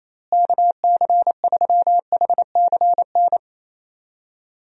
Morse code, heard as a keyed tone, about 28 wpm, 700 Hz. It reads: KC3HCN
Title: USA Call Signs - 6 Characters - Repeated 2x - 28wpm